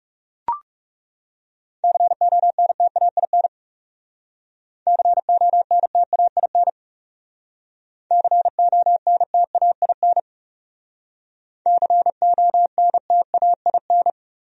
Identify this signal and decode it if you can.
Morse code, heard as a keyed tone, about 36 wpm, 700 Hz. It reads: CONTAIN CONTAIN CONTAIN CONTAIN
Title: Speed-Racing Top 500 Words 30wpm